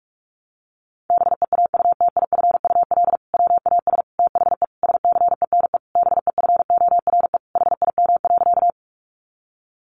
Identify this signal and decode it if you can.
Morse code, heard as a keyed tone, about 44 wpm, 700 Hz. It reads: BEAUTIFUL WAS THE SCENE BEFORE HIM.
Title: Sentences from Top 1100 Words 45wpm